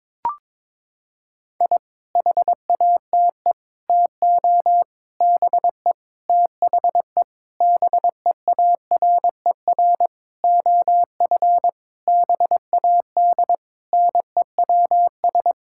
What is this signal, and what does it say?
Morse code, heard as a keyed tone, about 22 wpm, 700 Hz. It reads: I HATE TO BE THE BEARER OF BAD NEWS
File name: Sentences with 9-Word Phrases - 22wpm